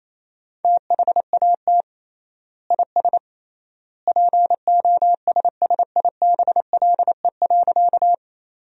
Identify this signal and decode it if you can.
Morse code, heard as a keyed tone, about 27 wpm, 700 Hz. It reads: THAT IS POSSIBLE.
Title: 3x Word-Spacing - Sentences from Top 500 Words - 28wpm